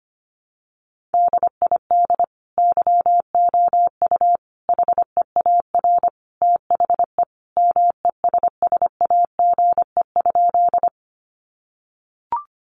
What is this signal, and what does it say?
Morse code, heard as a keyed tone, about 25 wpm, 700 Hz. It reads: DID YOU HEAR THE MESSAGE?
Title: Sentences from Top 1100 Words 25wpm